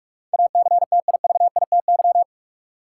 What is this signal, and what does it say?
Morse code, heard as a keyed tone, about 46 wpm, 700 Hz. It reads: ACTIVITY
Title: Sentences from Top 1000 Words - Review Part 4 - New Word Drill - 45wpm